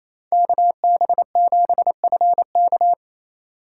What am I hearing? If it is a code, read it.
Morse code, heard as a keyed tone, about 28 wpm, 700 Hz. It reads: KB7FK
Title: USA Call Signs - 6x Character Spacing - 28wpm